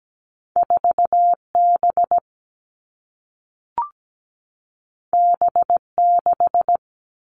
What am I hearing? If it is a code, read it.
Morse code, heard as a keyed tone, about 17 wpm, 700 Hz. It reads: 4B B6
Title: Two Challenging Letter-Number Combinations 17wpm